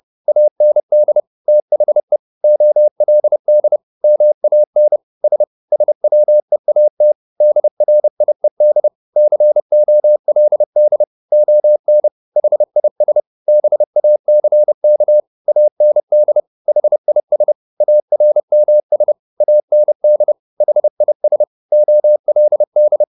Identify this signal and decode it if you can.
Morse code, heard as a keyed tone, about 30 wpm, 600 Hz. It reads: AND THE OLD MAN S SWEAT DRIED COLD ON HIS BACK AND HIS ARMS AND HIS OLD
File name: the old man and the sea / the old man and the sea 1 30wpm